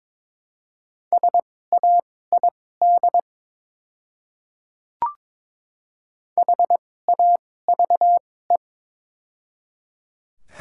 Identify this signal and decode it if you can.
Morse code, heard as a keyed tone, about 21 wpm, spaced as for 16 wpm, 700 Hz. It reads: SAID HAVE
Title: 2x Character Spacing - Top 300 Words - 22wpm